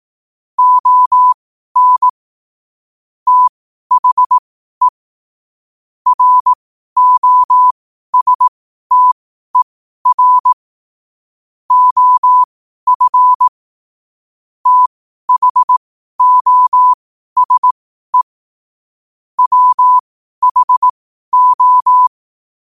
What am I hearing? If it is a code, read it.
Morse code, heard as a keyed tone, about 18 wpm, spaced as for 12 wpm, 1000 Hz. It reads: ON THE ROSTER OF THOSE WHO
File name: qotdfiles / 12wpm